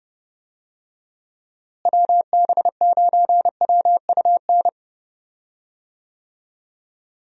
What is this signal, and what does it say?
Morse code, heard as a keyed tone, about 31 wpm, 700 Hz. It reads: WB9WUN
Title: USA Call Signs - 6 Characters - Repeated 2x - 30wpm